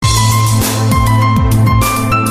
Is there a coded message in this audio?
No.